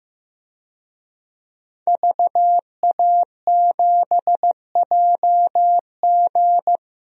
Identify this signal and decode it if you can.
Morse code, heard as a keyed tone, about 15 wpm, 700 Hz. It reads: VA7JG
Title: DXCC-100 Call Signs - 5 Characters - 15wpm